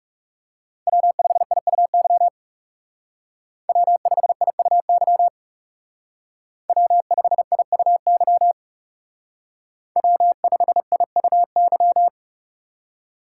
Morse code, heard as a keyed tone, about 38 wpm, 700 Hz. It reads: W5IUY W5IUY W5IUY W5IUY